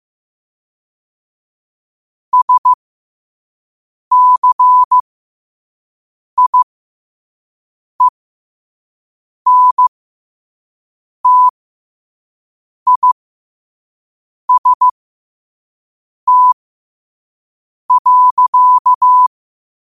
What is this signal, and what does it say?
Morse code, heard as a keyed tone, about 15 wpm, spaced as for 5 wpm, 1000 Hz. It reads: SCIENTIST.
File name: qotdfiles / 5wpm